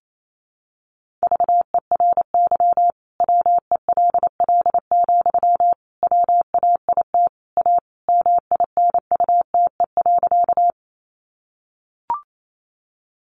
Morse code, heard as a keyed tone, about 28 wpm, 700 Hz. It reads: VERY WELL, WAIT A MINUTE.
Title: Sentences from Top 500 Words 28wpm